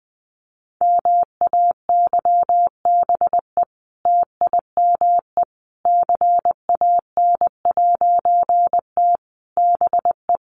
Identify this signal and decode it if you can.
Morse code, heard as a keyed tone, about 20 wpm, 700 Hz. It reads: MAYBE TIME CAN'T BE
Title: A Sound Of Thunder 20wpm